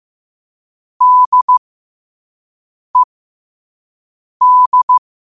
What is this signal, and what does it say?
Morse code, heard as a keyed tone, about 15 wpm, spaced as for 5 wpm, 1000 Hz. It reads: DED
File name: qotdfiles / 5wpm